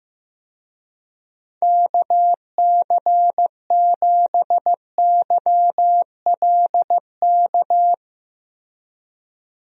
Morse code, heard as a keyed tone, about 15 wpm, 700 Hz. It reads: KC7YLK
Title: USA Call Signs - 6 Characters - Repeated 2x - 15wpm